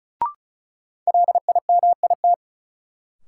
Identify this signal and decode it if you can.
Morse code, heard as a keyed tone, about 34 wpm, 700 Hz. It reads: LIMIT
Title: Sentences from Top 1500 Words - New Word Drill Alt - 35wpm